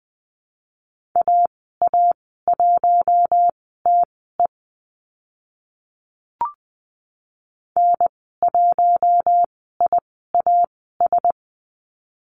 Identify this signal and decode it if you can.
Morse code, heard as a keyed tone, about 20 wpm, spaced as for 14 wpm, 700 Hz. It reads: AA1TE N1IAS
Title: 042 Lesson 8g - Number 1 - Call Signs 20wpm